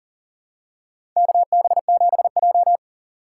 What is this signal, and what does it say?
Morse code, heard as a keyed tone, about 40 wpm, 700 Hz. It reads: KB7J
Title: USA Call Signs - 4 Characters - 4x Character Spacing - 40wpm